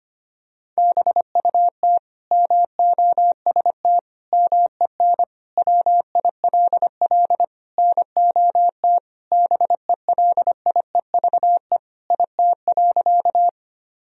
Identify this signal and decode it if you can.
Morse code, heard as a keyed tone, about 25 wpm, 700 Hz. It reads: BUT MOST MEN WILL NOT BELIEVE IT.
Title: Sentences from Top 1000 Words - Review Part 1 - 25wpm